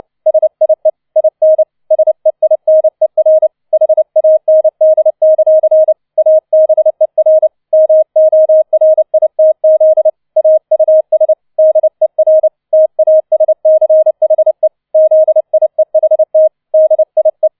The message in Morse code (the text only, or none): SIE IN SEINER HAND; ABER MORITZ AUS DER TASCHE ZIEHT DIE